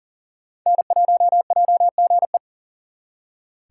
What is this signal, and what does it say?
Morse code, heard as a keyed tone, about 47 wpm, 700 Hz. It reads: N1JGE
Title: Speed Racing - USA 1x3 Call Signs 40wpm